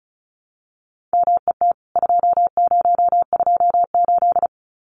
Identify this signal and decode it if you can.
Morse code, heard as a keyed tone, about 34 wpm, 700 Hz. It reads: MET 2028
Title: CWT Exchange 35wpm